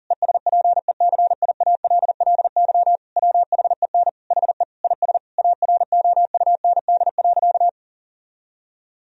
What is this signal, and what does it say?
Morse code, heard as a keyed tone, about 40 wpm, 700 Hz. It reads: ESPECIALLY WHEN HE IS AROUND.
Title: Sentences from Top 1000 Words - Review Part 4 - 40wpm